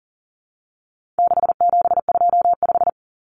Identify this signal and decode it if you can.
Morse code, heard as a keyed tone, about 40 wpm, 700 Hz. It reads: 6725
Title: Numbers - 4-Digit 40wpm